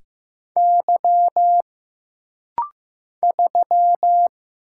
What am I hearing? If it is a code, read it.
Morse code, heard as a keyed tone, about 15 wpm, 700 Hz. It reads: Y 3
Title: Single Letter-Number Alt - Rapid-Fire 15wpm